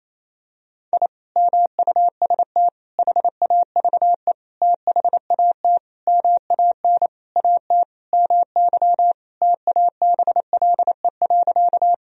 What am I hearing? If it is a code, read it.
Morse code, heard as a keyed tone, about 28 wpm, 700 Hz. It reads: I MUST HAVE THAT MAN AT MY TABLE.
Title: Sentences from Top 1000 Words - Review Part 2 - 28wpm